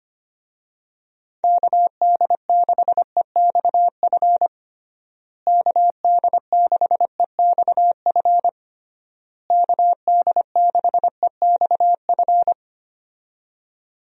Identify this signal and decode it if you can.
Morse code, heard as a keyed tone, about 25 wpm, 700 Hz. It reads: KD6EXF KD6EXF KD6EXF
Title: USA Call Signs - 6 Characters - Repeated 3x - 25wpm